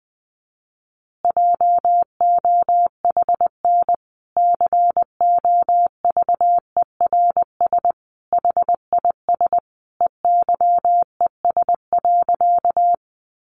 Morse code, heard as a keyed tone, about 20 wpm, 700 Hz. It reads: JOHN COVERS HIS EYES.
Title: Easy Phrases 20wpm